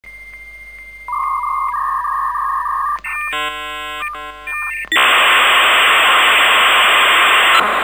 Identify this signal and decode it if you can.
Dial-up modem handshake.
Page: ностальгические звуки dial-up соединения